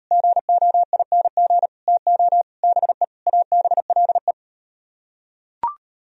Morse code, heard as a keyed tone, about 36 wpm, 700 Hz. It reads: GOING TO BE ABLE